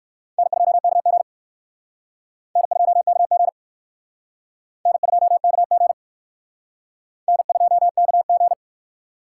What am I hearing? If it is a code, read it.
Morse code, heard as a keyed tone, about 62 wpm, 700 Hz. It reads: N2KG N2KG N2KG N2KG